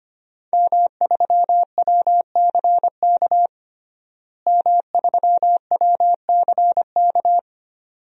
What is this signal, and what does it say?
Morse code, heard as a keyed tone, about 25 wpm, 700 Hz. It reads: M3WCK M3WCK
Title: DXCC-100 Call Signs - 5 Characters - Repeated 2x - 25wpm